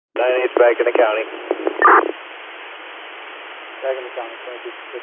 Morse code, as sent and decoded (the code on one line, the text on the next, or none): none